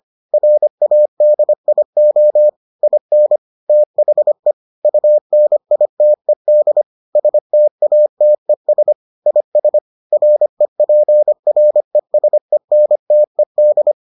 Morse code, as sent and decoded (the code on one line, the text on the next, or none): .-. .- -.. .. --- / .. -. / - .... / ..- -. .. - . -.. / ... - .- - . ... / .. ... / .-. .--. .-. ... -. - . -..
RADIO IN THE UNITED STATES IS REPRESENTED